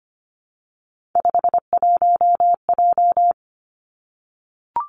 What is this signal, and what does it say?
Morse code, heard as a keyed tone, about 25 wpm, 700 Hz. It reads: 51J
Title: Three Challenging Letter-Number Combinations 25wpm